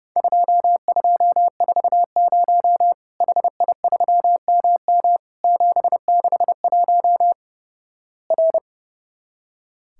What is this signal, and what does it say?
Morse code, heard as a keyed tone, about 32 wpm, 700 Hz. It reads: 2240 HI3MM 761 R
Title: Worked All Europe DX Contest - QTC Exchange - 30wpm